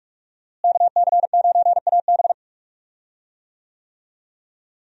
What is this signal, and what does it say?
Morse code, heard as a keyed tone, about 46 wpm, 700 Hz. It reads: KC9AB
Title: USA Call Signs - 5 Characters - Repeated 3x - 45wpm